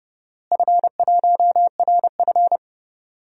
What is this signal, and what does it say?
Morse code, heard as a keyed tone, about 31 wpm, 700 Hz. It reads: F1RF
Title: DXCC-100 Call Signs - 4 Characters - Repeated 3x - 30wpm